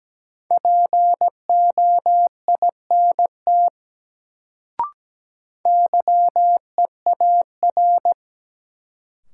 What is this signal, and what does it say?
Morse code, heard as a keyed tone, about 17 wpm, 700 Hz. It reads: POINT YEAR
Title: Sentences from Top 200 Words - New Word Drill - 17wpm